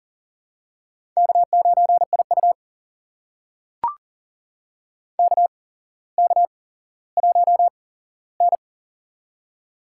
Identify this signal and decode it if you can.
Morse code, heard as a keyed tone, about 40 wpm, spaced as for 17 wpm, 700 Hz. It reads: K9IU KK1N
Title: USA Call Signs - 4 Characters - 8x Character Spacing - 40wpm